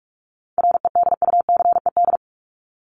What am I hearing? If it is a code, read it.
Morse code, heard as a keyed tone, about 44 wpm, 700 Hz. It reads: REDUCED
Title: ED Suffix - Top 100 Words with Base Words 45wpm